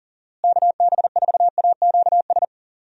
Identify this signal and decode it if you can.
Morse code, heard as a keyed tone, about 40 wpm, 700 Hz. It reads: KB4AQS
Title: USA Call Signs - 6 Characters - Repeated 2x - 40wpm